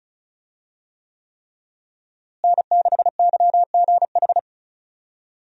Morse code, heard as a keyed tone, about 36 wpm, 700 Hz. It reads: N6YGH
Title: USA Call Signs - 5 Characters - Repeated 3x - 35wpm